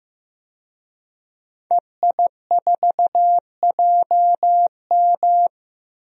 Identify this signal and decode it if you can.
Morse code, heard as a keyed tone, about 15 wpm, 700 Hz. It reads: EI4JM
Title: DXCC-100 Call Signs - 5 Characters - 15wpm